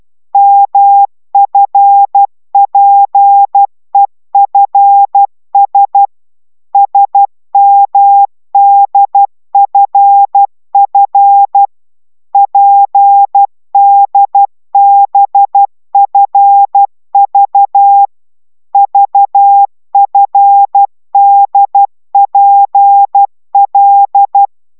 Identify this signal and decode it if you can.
Morse code, heard as a keyed tone, about 12 wpm, 800 Hz. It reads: MFPEFS SMDFF PDBFV VFDPL
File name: taviro tanulas / cw 69 10 10